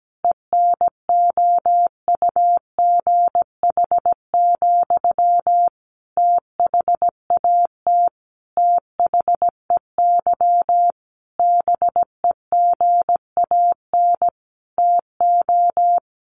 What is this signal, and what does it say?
Morse code, heard as a keyed tone, about 17 wpm, 700 Hz. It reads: ENOUGH, THAT THEY BEGAN TO
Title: Sentences from Top 800 Words 17wpm